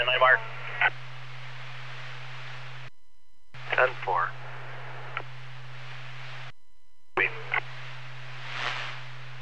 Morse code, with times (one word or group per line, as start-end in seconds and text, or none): none